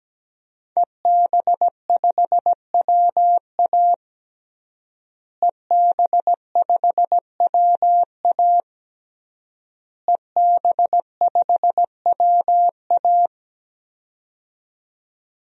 Morse code, heard as a keyed tone, about 17 wpm, 700 Hz. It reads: EB5WA EB5WA EB5WA
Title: DXCC-100 Call Signs - 5 Characters - Repeated 3x - 17wpm